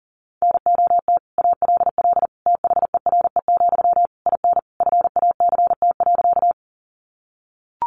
Morse code, heard as a keyed tone, about 40 wpm, 700 Hz. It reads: NOT ALL THERE, IN FACT.
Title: Sentences from Top 600 Words 40wpm